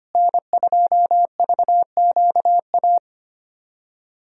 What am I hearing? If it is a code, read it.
Morse code, heard as a keyed tone, about 25 wpm, 700 Hz. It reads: N2VQA
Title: USA Call Signs - 5 Characters - Repeated 2x - 25wpm